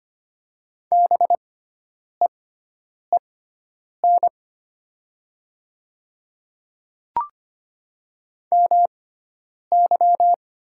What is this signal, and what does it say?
Morse code, heard as a keyed tone, about 25 wpm, spaced as for 9 wpm, 700 Hz. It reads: BEEN MY
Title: 6x Character Spacing - Top 300 Words - 25wpm